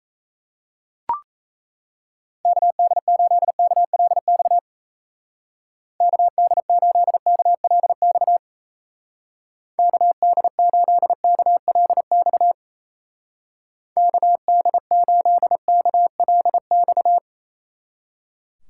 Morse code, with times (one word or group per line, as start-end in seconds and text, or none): 2.5-4.6 KD8KLX
6.0-8.4 KD8KLX
9.8-12.5 KD8KLX
14.0-17.2 KD8KLX